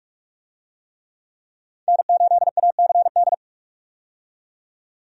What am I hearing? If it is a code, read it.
Morse code, heard as a keyed tone, about 46 wpm, 700 Hz. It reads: N8ACD
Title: USA Call Signs - 5 Characters - 45wpm